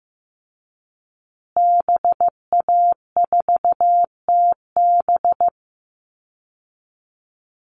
Morse code, heard as a keyed tone, about 15 wpm, 700 Hz. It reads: BA4TB
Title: Call Signs Repeated 2x - 15wpm